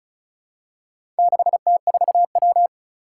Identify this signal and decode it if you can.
Morse code, heard as a keyed tone, about 36 wpm, 700 Hz. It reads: 6T4W
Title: Four Challenging Letter-Number Combinations 35wpm